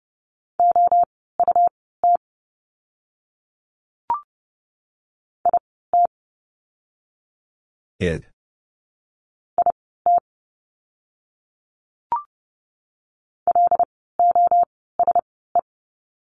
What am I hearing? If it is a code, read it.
Morse code, heard as a keyed tone, about 30 wpm, spaced as for 17 wpm, 700 Hz. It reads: OUT IT IT LOSE
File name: course / 30:10wpm / 108 Lesson 18f - Letter m - Words 30wpm